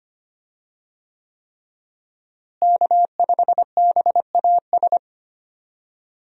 Morse code, heard as a keyed tone, about 25 wpm, 700 Hz. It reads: K5BAS
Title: USA Call Signs - 5 Characters - Repeated 3x - 25wpm